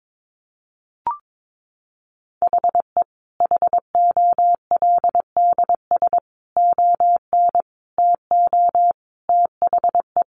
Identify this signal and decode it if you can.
Morse code, heard as a keyed tone, about 22 wpm, 700 Hz. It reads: HE HOLDS ON TO THE
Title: Additional Easy Phrases 22wpm